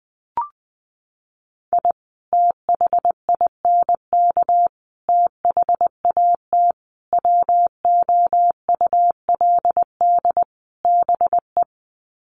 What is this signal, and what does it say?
Morse code, heard as a keyed tone, about 20 wpm, 700 Hz. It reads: I THINK THAT WOULD BE
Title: Sets of 5 Words Encore2 20wpm